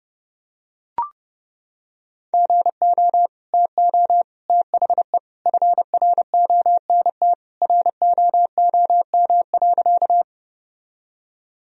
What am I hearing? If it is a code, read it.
Morse code, heard as a keyed tone, about 30 wpm, 700 Hz. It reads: GO TO THE FRONT ROOM.
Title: Sentences from Top 1000 Words - Review Part 2 - 30wpm